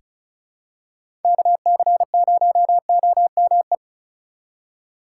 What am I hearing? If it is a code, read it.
Morse code, heard as a keyed tone, about 36 wpm, 700 Hz. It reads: KC0OME
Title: USA Call Signs - 6 Characters - 4x Character Spacing - 35wpm